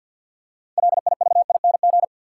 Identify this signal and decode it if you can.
Morse code, heard as a keyed tone, about 52 wpm, 700 Hz. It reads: LIVING
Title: Six Letter Words - 50wpm